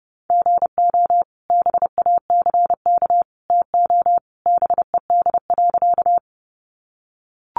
Morse code, heard as a keyed tone, about 30 wpm, 700 Hz. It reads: GO BACK TO BED.
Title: Sentences from Top 500 Words 30wpm